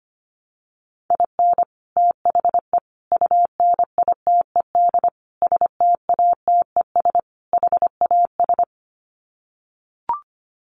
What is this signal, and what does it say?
Morse code, heard as a keyed tone, about 25 wpm, 700 Hz. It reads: IN THE UNITED STATES HAS
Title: Sets of 5 Words Encore2 25wpm